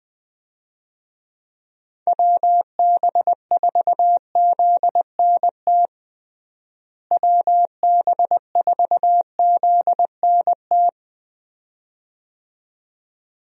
Morse code, heard as a keyed tone, about 20 wpm, 700 Hz. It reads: WB4ZNT WB4ZNT
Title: USA Call Signs - 6 Characters - Repeated 2x - 20wpm